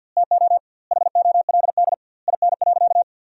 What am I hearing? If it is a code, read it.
Morse code, heard as a keyed tone, about 50 wpm, 700 Hz. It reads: TO HOLD IN.
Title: 2x Word-Spacing - Sentences from Top 400 Words - 50wpm